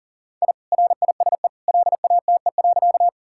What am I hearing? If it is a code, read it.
Morse code, heard as a keyed tone, about 40 wpm, 700 Hz. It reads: I RISE LATE.
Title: Sentences from Top 1000 Words - Review Part 6 - 40wpm